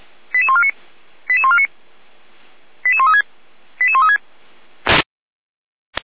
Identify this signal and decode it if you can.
Five-tone selective call (ZVEI), heard as quick runs of five tones, tones of about 70 ms. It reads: 88149 88149 88136 88136